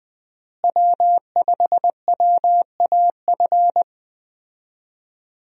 Morse code, heard as a keyed tone, about 20 wpm, 700 Hz. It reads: W5WAF
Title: USA Call Signs - 5 Characters - Repeated 3x - 20wpm